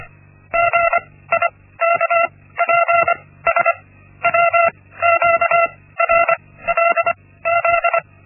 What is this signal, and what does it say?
Morse code, heard as a keyed tone, about 25 wpm, spaced as for 16 wpm, 680 Hz. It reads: GIKPS WQRLZ
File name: Linksmakalnis-1